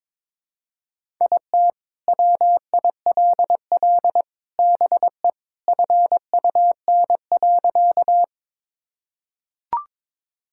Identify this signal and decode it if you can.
Morse code, heard as a keyed tone, about 21 wpm, 700 Hz. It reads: IT WILL BE FUN.
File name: practice / 22wpm / Sentences with 3-Word Phrases - 22wpm